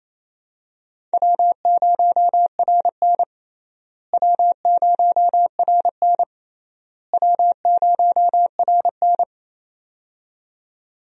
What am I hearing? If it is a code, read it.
Morse code, heard as a keyed tone, about 28 wpm, 700 Hz. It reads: W0RN W0RN W0RN
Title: USA Call Signs - 4 Characters - Repeated 3x - 28wpm